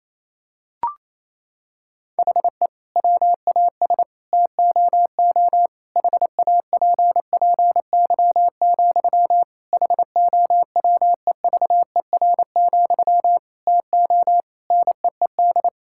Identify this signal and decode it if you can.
Morse code, heard as a keyed tone, about 28 wpm, 700 Hz. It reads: HE WAS TOO HAPPY, HOWEVER, TO NEED
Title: Sentences from Top 1000 Words - Review Part 6 - 28wpm